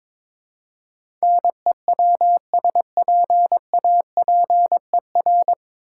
Morse code, heard as a keyed tone, about 22 wpm, 690 Hz. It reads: NEWSPAPER
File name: Easy Words - Speed-Racing with Hints - 22wpm